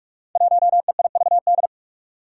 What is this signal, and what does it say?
Morse code, heard as a keyed tone, about 46 wpm, 700 Hz. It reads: 1EIVD
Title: Five Challenging Letter-Number Combinations 45wpm